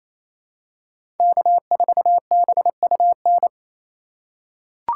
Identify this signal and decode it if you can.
Morse code, heard as a keyed tone, about 28 wpm, 700 Hz. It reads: K4BUN
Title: USA Call Signs - 5 Characters - 2x Character Spacing - 28wpm